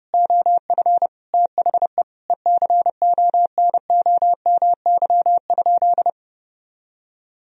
Morse code, heard as a keyed tone, about 32 wpm, 700 Hz. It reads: OF THE ECONOMY?
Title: Sentences from Top 1000 Words - Review Part 3 - 30wpm